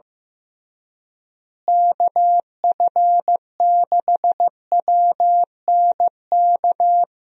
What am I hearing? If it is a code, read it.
Morse code, heard as a keyed tone, about 15 wpm, 700 Hz. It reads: KF6WNK
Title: USA Call Signs - 6 Characters - Repeated 3x - 15wpm